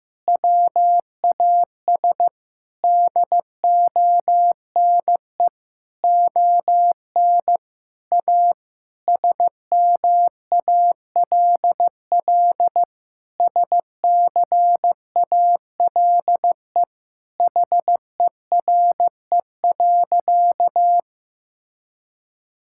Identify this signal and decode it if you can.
Morse code, heard as a keyed tone, about 15 wpm, 700 Hz. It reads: WAS DONE ON A SMALL SCALE HERE.
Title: Sentences from Top 1000 Words - Review Part 5 - 15wpm